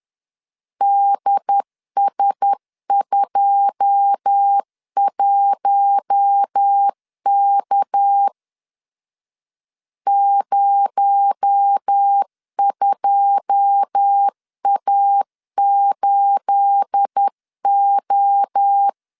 Morse code, heard as a keyed tone, about 10 wpm, 800 Hz. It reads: DS21K 02A8O
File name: LEZIONI / 9 28 10wpm 2842